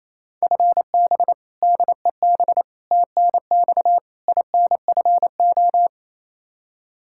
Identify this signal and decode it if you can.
Morse code, heard as a keyed tone, about 27 wpm, 700 Hz. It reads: FB DEB TNX INFO
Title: K1USN SST QSO Practice 28wpm